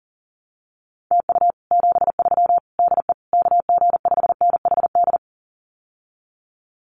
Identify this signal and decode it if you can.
Morse code, heard as a keyed tone, about 38 wpm, 700 Hz. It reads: TU 73 DE KG5NHD
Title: Common QSO Phrases Encore 40wpm